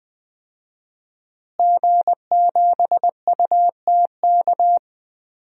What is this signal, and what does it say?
Morse code, heard as a keyed tone, about 20 wpm, 700 Hz. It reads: G7UTK
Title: DXCC-100 Call Signs - 5 Characters - Repeated 2x - 20wpm